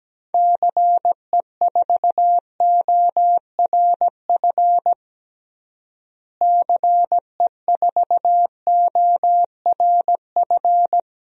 Morse code, heard as a keyed tone, about 17 wpm, 700 Hz. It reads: CE4ORF CE4ORF
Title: DXCC-100 Call Signs - 6 Characters - Repeated 3x - 17wpm